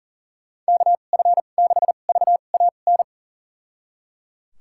Morse code, heard as a keyed tone, about 38 wpm, spaced as for 28 wpm, 700 Hz. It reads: KF6VAN
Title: USA Call Signs - 6 Characters - 2x Character Spacing - 40wpm